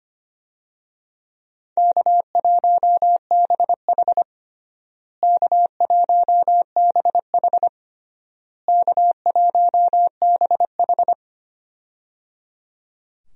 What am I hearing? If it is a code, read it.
Morse code, heard as a keyed tone, about 25 wpm, 700 Hz. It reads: K1BH K1BH K1BH
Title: USA Call Signs - 4 Characters - Repeated 3x - 25wpm